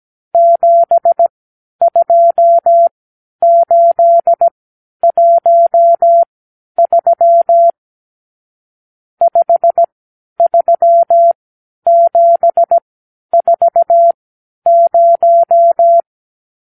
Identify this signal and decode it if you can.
Morse code, heard as a keyed tone, about 17 wpm, spaced as for 10 wpm, 680 Hz. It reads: 72813 53740
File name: arrl / Review (1-0)